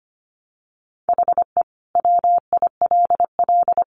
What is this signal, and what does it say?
Morse code, heard as a keyed tone, about 25 wpm, 700 Hz. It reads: HE WILL
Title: Sentences from Top 1200 Words 25wpm